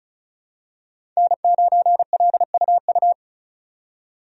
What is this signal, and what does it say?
Morse code, heard as a keyed tone, about 36 wpm, 700 Hz. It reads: N9LUU
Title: USA Call Signs - 5 Characters - 35wpm